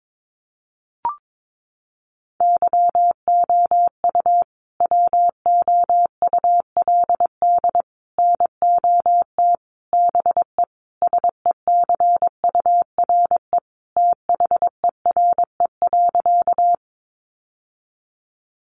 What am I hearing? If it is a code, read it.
Morse code, heard as a keyed tone, about 22 wpm, 700 Hz. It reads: YOU WOULD NOT BE SECURE THERE.
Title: Sentences from Top 1200 Words 22wpm